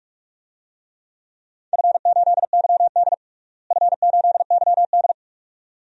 Morse code, heard as a keyed tone, about 41 wpm, 700 Hz. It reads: F8YD F8YD
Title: DXCC-100 Call Signs - 4 Characters - Repeated 2x - 45wpm